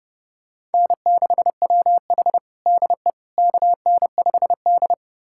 Morse code, heard as a keyed tone, about 30 wpm, 700 Hz. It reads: N6WH DE KN5D
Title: USA Callsign DE Callsign - 4 Characters - 30wpm